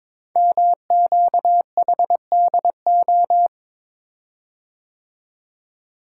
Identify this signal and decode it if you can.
Morse code, heard as a keyed tone, about 22 wpm, 700 Hz. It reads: MQHDO
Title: Code Groups - 5 Letters - 22wpm